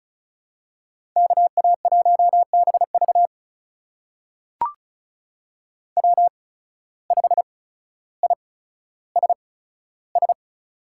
Morse code, heard as a keyed tone, about 36 wpm, spaced as for 15 wpm, 700 Hz. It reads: KA1BV W5ISS
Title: USA Call Signs - 8x Character Spacing - 35wpm